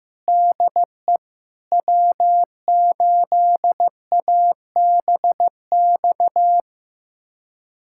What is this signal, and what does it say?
Morse code, heard as a keyed tone, about 15 wpm, 700 Hz. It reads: DE W8ABX
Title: USA Callsign DE Callsign 15wpm